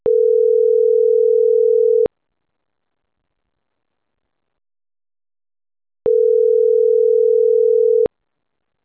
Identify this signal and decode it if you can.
Ringback tone (North America), 440 + 480 Hz.